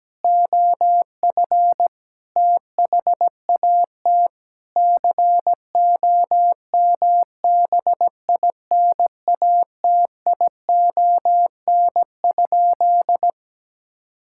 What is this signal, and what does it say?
Morse code, heard as a keyed tone, about 17 wpm, 700 Hz. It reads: OF THAT COMBINATION?
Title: Sentences from Top 1000 Words - Review Part 6 - 17wpm